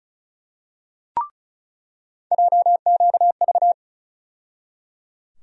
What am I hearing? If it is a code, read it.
Morse code, heard as a keyed tone, about 36 wpm, 700 Hz. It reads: JQV
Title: USA Call Sign Postfixes - 3 Letters 35wpm